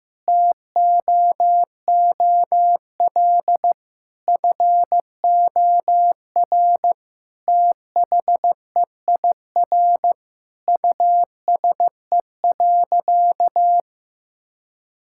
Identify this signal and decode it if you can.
Morse code, heard as a keyed tone, about 16 wpm, 700 Hz. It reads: TOOL FOR THEIR USE.
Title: Sentences from Top 1000 Words - Review Part 6 - 15wpm